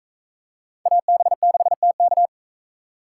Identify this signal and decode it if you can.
Morse code, heard as a keyed tone, about 38 wpm, 700 Hz. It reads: AB6TK